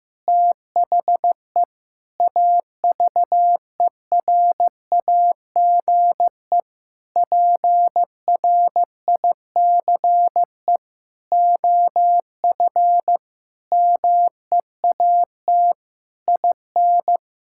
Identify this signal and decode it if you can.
Morse code, heard as a keyed tone, about 15 wpm, 700 Hz. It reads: THE AVERAGE PRICE OF MEAT IN
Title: Sentences from Top 1000 Words - Review Part 6 - 15wpm